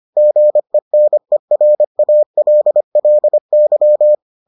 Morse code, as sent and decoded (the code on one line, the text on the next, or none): --. -. .-. .- .-.. .-.. -.--
GENERALLY